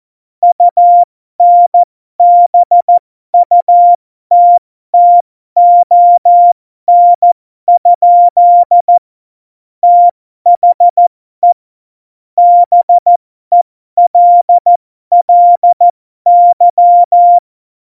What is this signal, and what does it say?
Morse code, heard as a keyed tone, about 14 wpm, spaced as for 12 wpm, 700 Hz. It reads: UNBUTTON? THE BELLY
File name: Morsekurs / Witze